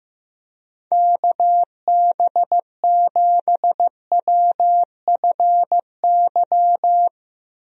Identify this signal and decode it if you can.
Morse code, heard as a keyed tone, about 15 wpm, 700 Hz. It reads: KB7WFY
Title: USA Call Signs - 6 Characters - Repeated 2x - 15wpm